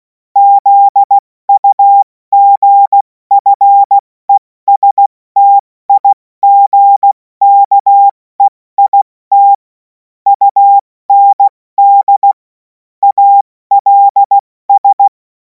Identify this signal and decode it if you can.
Morse code, heard as a keyed tone, about 16 wpm, 800 Hz. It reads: ZUGFESTIGKEIT UND ALS
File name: Morsekurs / Witze / Witz 10. Februar 2026 70 F80